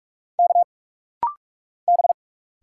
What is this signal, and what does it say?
Morse code, heard as a keyed tone, about 46 wpm, 690 Hz. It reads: K B
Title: Single Letters - Rapid-Fire 45wpm